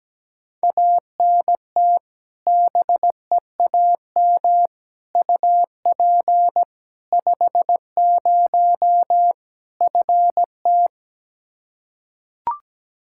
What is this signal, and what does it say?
Morse code, heard as a keyed tone, about 17 wpm, 700 Hz. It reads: ANT BEAM UP 50 FT
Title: K1USN SST QSO Practice 17wpm